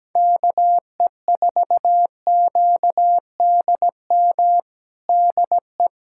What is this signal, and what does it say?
Morse code, heard as a keyed tone, about 17 wpm, 700 Hz. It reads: KE4QDM DE
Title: USA Callsign DE Callsign - 6 Characters - 17wpm